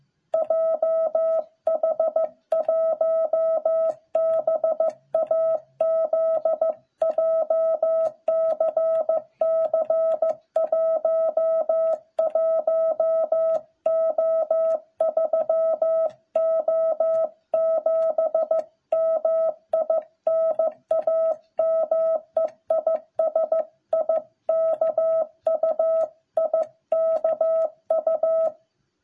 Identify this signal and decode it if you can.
Morse code, heard as a keyed tone, about 15 wpm, 660 Hz. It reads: JH1BAZ JCC 11O3O7 MI NAME IS IKU IKU